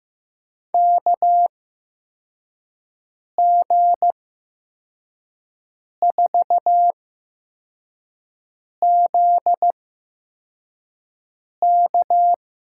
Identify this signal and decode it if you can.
Morse code, heard as a keyed tone, about 15 wpm, spaced as for 4 wpm, 700 Hz. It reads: KG4ZK
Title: USA Call Signs - 6 Characters - 8x Character Spacing - 15wpm